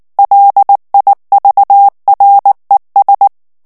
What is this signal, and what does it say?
Morse code, heard as a keyed tone, about 19 wpm, 800 Hz. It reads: LIVRES
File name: FRANCAIS TEXTE CLAIR 2 VITESSE 18 MOTS MINUTES